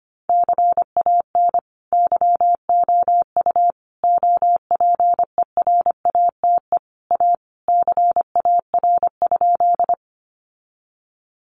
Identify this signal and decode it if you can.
Morse code, heard as a keyed tone, about 25 wpm, 700 Hz. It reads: CAN YOU OPERATE A CAR?
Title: Sentences from Top 1500 Words 25wpm